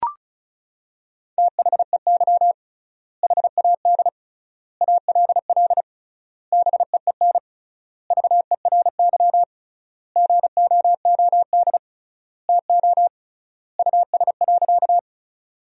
Morse code, heard as a keyed tone, about 36 wpm, 700 Hz. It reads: THEY HAD ALL BEEN VERY GOOD TO US.
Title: 3x Word-Spacing - Sentences from Top 200 Words - 35wpm